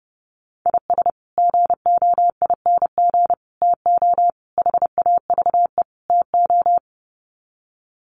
Morse code, heard as a keyed tone, about 28 wpm, 700 Hz. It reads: IS GOING TO HAVE TO
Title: Sets of 5 Words 30wpm